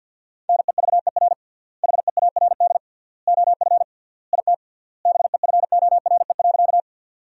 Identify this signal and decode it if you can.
Morse code, heard as a keyed tone, about 52 wpm, 700 Hz. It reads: NEVER HEARD OF IT BEFORE.
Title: 3x Word-Spacing - Sentences from Top 400 Words - 50wpm